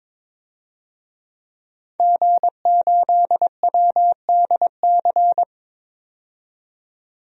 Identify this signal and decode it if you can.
Morse code, heard as a keyed tone, about 22 wpm, 700 Hz. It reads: G8WDC
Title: DXCC-100 Call Signs - 5 Characters - 22wpm